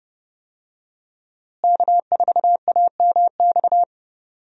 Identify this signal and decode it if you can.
Morse code, heard as a keyed tone, about 30 wpm, 700 Hz. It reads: K4AMX
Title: USA Call Signs - 5 Characters - Repeated 2x - 30wpm